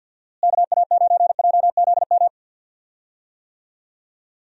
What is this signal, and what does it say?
Morse code, heard as a keyed tone, about 46 wpm, 700 Hz. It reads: KA9JZM
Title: USA Call Signs - 6 Characters - Repeated 2x - 50wpm